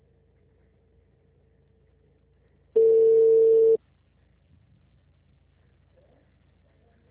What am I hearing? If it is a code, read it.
Ringback tone (China, 450 Hz).